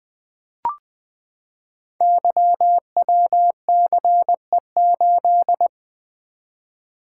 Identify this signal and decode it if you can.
Morse code, heard as a keyed tone, about 20 wpm, 700 Hz. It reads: YWCE8
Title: Code Groups - 5 Letter:Numbers - 20wpm